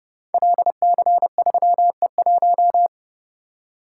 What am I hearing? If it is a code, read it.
Morse code, heard as a keyed tone, about 28 wpm, 700 Hz. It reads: LC3E1
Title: Code Groups - 5 Letter:Numbers - 30wpm